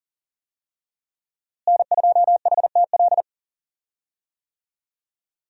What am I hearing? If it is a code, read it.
Morse code, heard as a keyed tone, about 40 wpm, 700 Hz. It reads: N2HTL
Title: USA Call Signs - 5 Characters - Repeated 3x - 40wpm